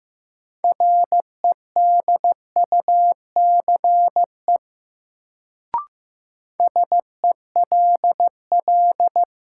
Morse code, heard as a keyed tone, about 15 wpm, 700 Hz. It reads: REDUCE SELL
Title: Sentences from Top 1100 Words - New Word Drill Alt - 15wpm